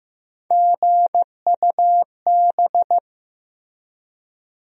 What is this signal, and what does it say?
Morse code, heard as a keyed tone, about 15 wpm, 700 Hz. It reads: GUB
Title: USA Call Sign Postfixes - 3 Letters 15wpm